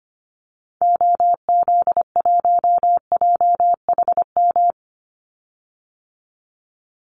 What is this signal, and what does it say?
Morse code, heard as a keyed tone, about 25 wpm, 700 Hz. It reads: OZ1JHM
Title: Call Signs - 25wpm